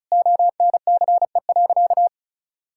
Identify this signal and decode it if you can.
Morse code, heard as a keyed tone, about 34 wpm, 690 Hz. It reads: ONCE.
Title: Sentences from Top 1000 Words - Review Part 6 - 35wpm